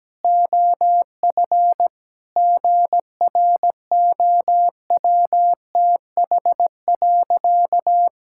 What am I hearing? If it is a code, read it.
Morse code, heard as a keyed tone, about 17 wpm, 700 Hz. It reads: OF GROWTH.
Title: Sentences from Top 1000 Words - Review Part 5 - 17wpm